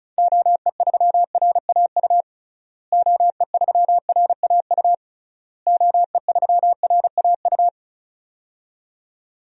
Morse code, heard as a keyed tone, about 34 wpm, 700 Hz. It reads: OE3RAU OE3RAU OE3RAU
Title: DXCC-100 Call Signs - 6 Characters - Repeated 3x - 35wpm